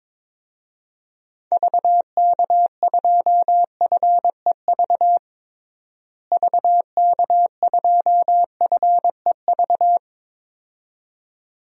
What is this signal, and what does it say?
Morse code, heard as a keyed tone, about 22 wpm, 700 Hz. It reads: VK2FEV VK2FEV
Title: DXCC-100 Call Signs - 6 Characters - Repeated 2x - 22wpm